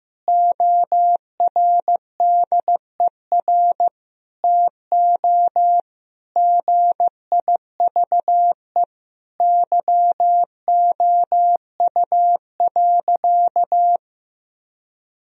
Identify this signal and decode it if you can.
Morse code, heard as a keyed tone, about 15 wpm, 700 Hz. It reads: ORDER TO GIVE YOU.
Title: Sentences from Top 1000 Words - Review Part 2 - 15wpm